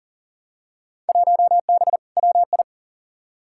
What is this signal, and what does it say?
Morse code, heard as a keyed tone, about 40 wpm, 700 Hz. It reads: 1B WI
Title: Field Day Exchanges Encore - 40wpm